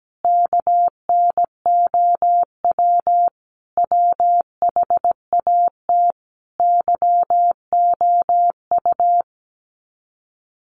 Morse code, heard as a keyed tone, about 17 wpm, 700 Hz. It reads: KNOW WHAT YOU
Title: Sets of 5 Words 17wpm